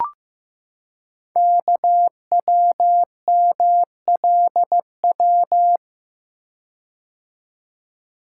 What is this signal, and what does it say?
Morse code, heard as a keyed tone, about 15 wpm, 700 Hz. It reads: KWMLW
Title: Code Groups - 5 Letters - 15wpm